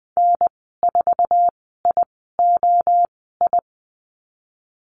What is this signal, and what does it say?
Morse code, heard as a keyed tone, about 20 wpm, spaced as for 15 wpm, 700 Hz. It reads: N4IOI
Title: 076 Lesson 13g - Letter L - Call Signs 20wpm